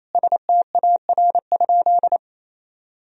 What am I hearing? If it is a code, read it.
Morse code, heard as a keyed tone, about 28 wpm, 700 Hz. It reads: STAR?
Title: Sentences from Top 1000 Words - Review Part 2 - 28wpm